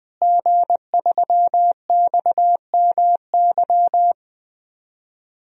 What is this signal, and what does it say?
Morse code, heard as a keyed tone, about 20 wpm, 700 Hz. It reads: G3XMY
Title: DXCC-100 Call Signs - 5 Characters - Repeated 2x - 20wpm